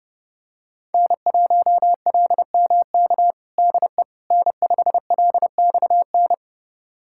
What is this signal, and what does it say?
Morse code, heard as a keyed tone, about 30 wpm, 700 Hz. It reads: N1LMK DE N5LXN
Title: USA Callsign DE Callsign - 5 Characters - 30wpm